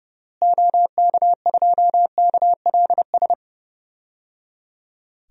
Morse code, heard as a keyed tone, about 30 wpm, 700 Hz. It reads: OK2KLS
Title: DXCC-100 Call Signs - 6 Characters - Repeated 3x - 30wpm